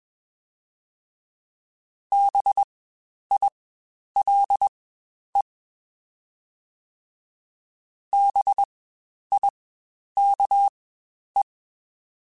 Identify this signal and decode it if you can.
Morse code, heard as a keyed tone, about 21 wpm, spaced as for 10 wpm, 800 Hz. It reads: BILE BIKE
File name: Spaced Alphas / Spaced Alphas mp3s / Bs sp @20